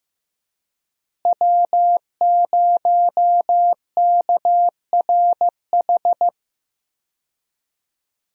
Morse code, heard as a keyed tone, about 15 wpm, 700 Hz. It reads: W0KRH
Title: USA Call Signs - 5 Characters - Repeated 2x - 15wpm